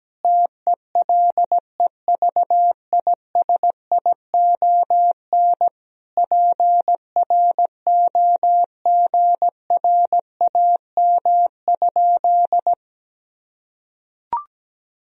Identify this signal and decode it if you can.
Morse code, heard as a keyed tone, about 17 wpm, 700 Hz. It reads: TELEVISION PROGRAM?
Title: Sentences from Top 1000 Words - Review Part 4 - 17wpm